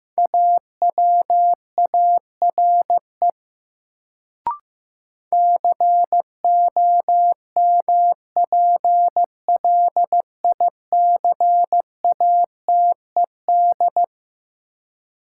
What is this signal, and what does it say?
Morse code, heard as a keyed tone, about 15 wpm, 700 Hz. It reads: AWARE COMPLICATED
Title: Sentences from Top 1200 Words - New Word Drill - 15wpm